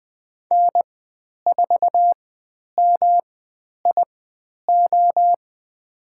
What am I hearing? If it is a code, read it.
Morse code, heard as a keyed tone, about 20 wpm, spaced as for 10 wpm, 700 Hz. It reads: N4MIO
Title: 109 Lesson 18g - Letter m - Call Signs 20wpm